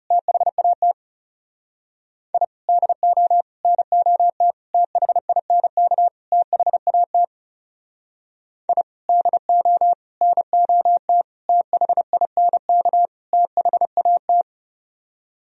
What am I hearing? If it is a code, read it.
Morse code, heard as a keyed tone, about 37 wpm, 700 Hz. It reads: THAT I DO NOT THINK THAT I DO NOT THINK THAT
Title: Speed-Racing - Sets of 5 Words 30wpm